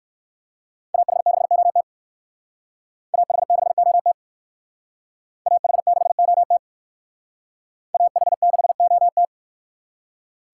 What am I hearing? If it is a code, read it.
Morse code, heard as a keyed tone, about 56 wpm, 700 Hz. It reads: AH6OT AH6OT AH6OT AH6OT
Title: Speed Racing - USA 2x2 Call Signs 45wpm